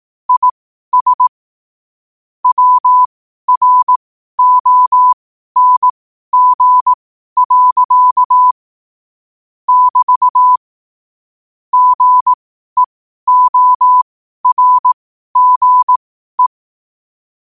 Morse code, heard as a keyed tone, about 18 wpm, spaced as for 12 wpm, 1000 Hz. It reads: IS WRONG. <BT> GEORGE
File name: qotdfiles / 12wpm